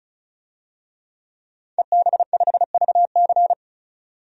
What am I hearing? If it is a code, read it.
Morse code, heard as a keyed tone, about 34 wpm, 700 Hz. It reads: EB5VC
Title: DXCC-100 Call Signs - 5 Characters - 35wpm